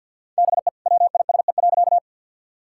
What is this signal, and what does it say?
Morse code, heard as a keyed tone, about 50 wpm, 700 Hz. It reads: BE WISE.
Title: Sentences from Top 1600 Words 50wpm